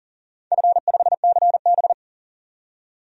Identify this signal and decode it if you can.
Morse code, heard as a keyed tone, about 40 wpm, 690 Hz. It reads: F5CB
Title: DXCC-100 Call Signs - 4 Characters - 40wpm